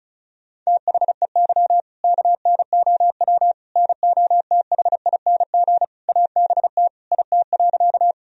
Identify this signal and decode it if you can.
Morse code, heard as a keyed tone, about 34 wpm, 700 Hz. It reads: THEY KNOW NOTHING ABT IT.
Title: Sentences from Top 500 Words and Abbreviations 35wpm